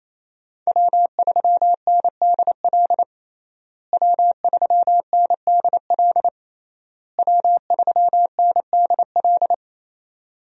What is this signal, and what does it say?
Morse code, heard as a keyed tone, about 28 wpm, 700 Hz. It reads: W3NDL W3NDL W3NDL
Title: USA Call Signs - 5 Characters - Repeated 3x - 28wpm